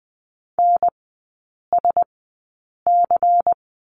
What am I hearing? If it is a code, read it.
Morse code, heard as a keyed tone, about 20 wpm, spaced as for 8 wpm, 700 Hz. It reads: NSC
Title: 093 Lesson 16e - Letter c - Three Characters 20wpm